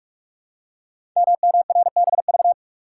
Morse code, heard as a keyed tone, about 44 wpm, 690 Hz. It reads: MMRBV
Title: Code Groups - 5 Letters - 45wpm